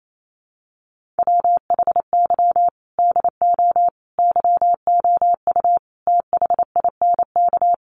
Morse code, heard as a keyed tone, about 28 wpm, 700 Hz. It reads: WHY DO YOU THINK
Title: Sets of 5 Words Encore2 28wpm